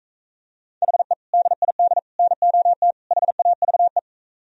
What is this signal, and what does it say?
Morse code, heard as a keyed tone, about 45 wpm, 700 Hz. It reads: HE DID NOT HAVE